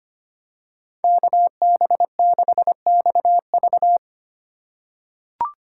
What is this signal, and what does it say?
Morse code, heard as a keyed tone, about 25 wpm, 700 Hz. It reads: KB6XV
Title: USA Call Signs - 5 Characters - 2x Character Spacing - 25wpm